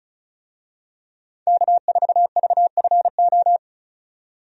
Morse code, heard as a keyed tone, about 34 wpm, 700 Hz. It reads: K4VFO
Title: USA Call Signs - 5 Characters - 35wpm